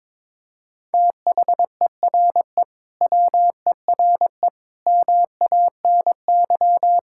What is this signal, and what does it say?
Morse code, heard as a keyed tone, about 22 wpm, 700 Hz. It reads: THERE WERE MANY
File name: practice / 22wpm / Sentences with 2-Word Phrases - 22wpm